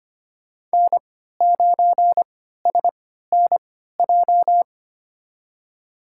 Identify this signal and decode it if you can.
Morse code, heard as a keyed tone, about 25 wpm, spaced as for 14 wpm, 700 Hz. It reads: N9SNJ
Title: USA Call Signs - 5 Characters - 3x Character Spacing - 25wpm